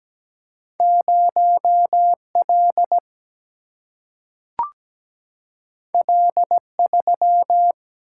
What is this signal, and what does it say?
Morse code, heard as a keyed tone, about 17 wpm, 700 Hz. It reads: 0 L L 3
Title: Single-Letter Single-Number 17wpm